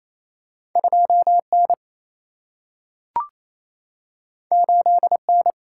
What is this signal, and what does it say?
Morse code, heard as a keyed tone, about 28 wpm, 700 Hz. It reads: 2 N 8 N
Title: Single-Letter Single-Number 28wpm